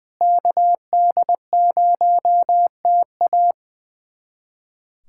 Morse code, heard as a keyed tone, about 20 wpm, 700 Hz. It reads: KD0TA
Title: USA Callsign DE Callsign 20wpm